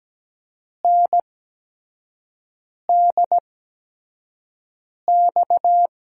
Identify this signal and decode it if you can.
Morse code, heard as a keyed tone, about 17 wpm, spaced as for 5 wpm, 700 Hz. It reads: NDX
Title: USA Call Signs - 6 Characters - 8x Character Spacing - 17wpm